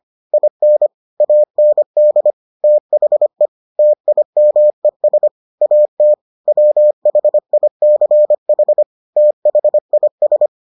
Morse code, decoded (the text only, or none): IN AND THE TIMES AT WHICH THIS